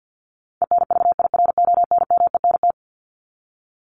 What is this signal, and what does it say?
Morse code, heard as a keyed tone, about 48 wpm, 700 Hz. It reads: ENVIRONMENT
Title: Top 501-750 Words 50wpm